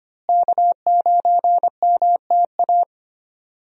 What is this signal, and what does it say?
Morse code, heard as a keyed tone, about 25 wpm, 700 Hz. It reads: K9MTA
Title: USA Call Signs - 5 Characters - Repeated 2x - 25wpm